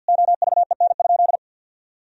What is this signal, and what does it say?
Morse code, heard as a keyed tone, about 50 wpm, 700 Hz. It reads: OVEN?
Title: Sentences from Top 1000 Words - Review Part 4 - 50wpm